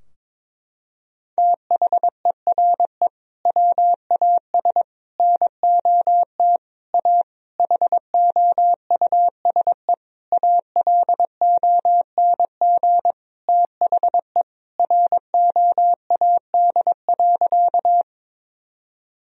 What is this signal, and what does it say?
Morse code, heard as a keyed tone, about 22 wpm, 700 Hz. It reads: THERE WAS NOT A HOUSE ALONG THE ROAD.
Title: Sentences from Top 1000 Words - Review Part 2 - 22wpm